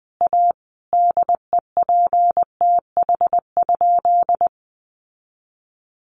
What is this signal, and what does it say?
Morse code, heard as a keyed tone, about 20 wpm, 700 Hz. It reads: A DEPTH?
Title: Sentences from Top 1200 Words 20wpm